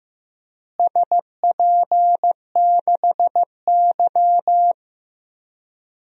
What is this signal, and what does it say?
Morse code, heard as a keyed tone, about 15 wpm, 700 Hz. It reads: SP6Y
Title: DXCC-100 Call Signs - 4 Characters - Repeated 2x - 15wpm